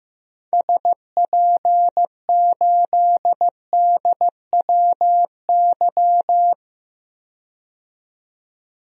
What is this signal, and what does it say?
Morse code, heard as a keyed tone, about 16 wpm, 700 Hz. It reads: SP8DWY
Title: DXCC-100 Call Signs - 6 Characters - 15wpm